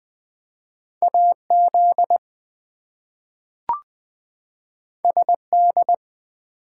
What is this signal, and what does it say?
Morse code, heard as a keyed tone, about 20 wpm, 700 Hz. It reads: AZ SD
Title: US States and Canadian Provinces 20wpm